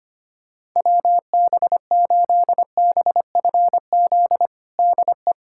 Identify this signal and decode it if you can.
Morse code, heard as a keyed tone, about 25 wpm, 700 Hz. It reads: WB8BFZ DE
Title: USA Callsign DE Callsign - 6 Characters - 25wpm